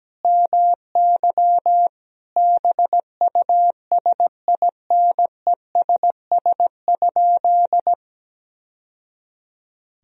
Morse code, heard as a keyed tone, about 17 wpm, 700 Hz. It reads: MY BUSINESS?
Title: Sentences from Top 1000 Words - Review Part 1 - 17wpm